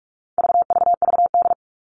Morse code, heard as a keyed tone, about 44 wpm, 700 Hz. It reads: VVVD
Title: All Cut Numbers - 4-Digit 45wpm